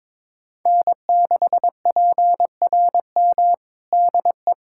Morse code, heard as a keyed tone, about 22 wpm, 700 Hz. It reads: N6PRM DE
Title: USA Callsign DE Callsign - 5 Characters - 22wpm